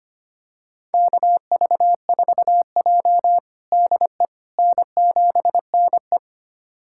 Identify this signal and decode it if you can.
Morse code, heard as a keyed tone, about 25 wpm, 700 Hz. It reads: KV4J DE N7NE
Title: USA Callsign DE Callsign - 4 Characters - 25wpm